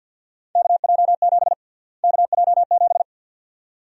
Morse code, heard as a keyed tone, about 53 wpm, 700 Hz. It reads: KJ7 KJ7
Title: USA Call Sign Prefixes - Repeated 2x - 50wpm